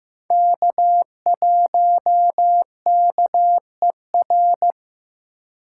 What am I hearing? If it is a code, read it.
Morse code, heard as a keyed tone, about 15 wpm, 700 Hz. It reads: K1KER
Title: USA Call Signs - 5 Characters - 2x Character Spacing - 15wpm